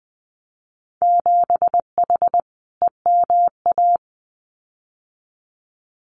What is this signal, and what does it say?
Morse code, heard as a keyed tone, about 20 wpm, 700 Hz. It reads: 7H EMA
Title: Winter Field Day Exchanges Encore 20wpm